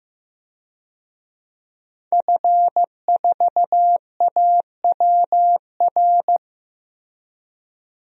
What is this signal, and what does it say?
Morse code, heard as a keyed tone, about 15 wpm, 700 Hz. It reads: F4AWR
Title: DXCC-100 Call Signs - 5 Characters - Repeated 2x - 15wpm